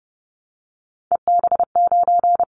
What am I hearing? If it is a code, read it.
Morse code, heard as a keyed tone, about 32 wpm, 700 Hz. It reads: EB9
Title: Callsign Prefixes for Top 100 DXCC - 30wpm